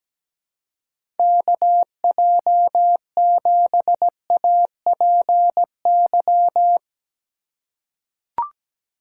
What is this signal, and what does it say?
Morse code, heard as a keyed tone, about 17 wpm, 700 Hz. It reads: KJ7APY
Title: USA Call Signs - 6 Characters - 2x Character Spacing - 17wpm